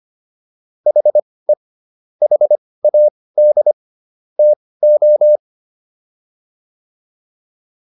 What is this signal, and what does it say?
Morse code, heard as a keyed tone, about 25 wpm, spaced as for 18 wpm, 600 Hz. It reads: HE HAD TO